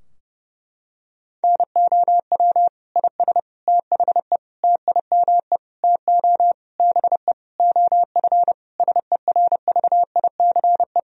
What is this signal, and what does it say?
Morse code, heard as a keyed tone, about 32 wpm, 700 Hz. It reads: NOW IS THE TIME TO BE OF SERVICE
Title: Sentences from Top 1000 Words - Review Part 4 - 30wpm